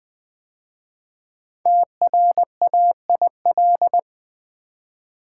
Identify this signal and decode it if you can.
Morse code, heard as a keyed tone, about 20 wpm, 700 Hz. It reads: TRAIL
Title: 5 Letter Words - First 3 Letters Given - 20wpm